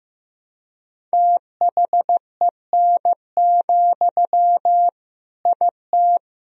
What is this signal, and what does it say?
Morse code, heard as a keyed tone, about 15 wpm, 700 Hz. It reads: THEN, IT
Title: Sentences from Top 1000 Words - Review Part 1 - 15wpm